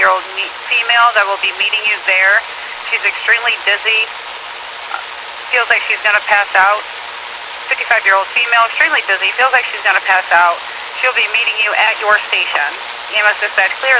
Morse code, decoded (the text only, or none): none